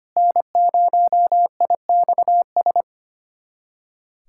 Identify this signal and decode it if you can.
Morse code, heard as a keyed tone, about 25 wpm, 700 Hz. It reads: N0IXS
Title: USA Callsign DE Callsign - 5 Characters - 25wpm